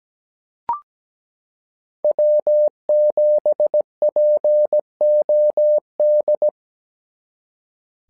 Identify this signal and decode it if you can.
Morse code, heard as a keyed tone, about 17 wpm, 600 Hz. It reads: W7POD
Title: SOTA QSO Practice 17wpm